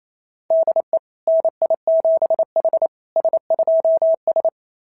Morse code, heard as a keyed tone, about 28 wpm, 690 Hz. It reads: DE NI7H S2S
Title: SOTA QSO Practice 28wpm